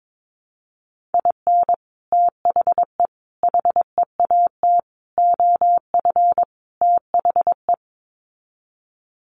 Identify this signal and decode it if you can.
Morse code, heard as a keyed tone, about 22 wpm, 700 Hz. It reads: IN THE HEAT OF THE
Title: Sets of 5 Words Encore2 22wpm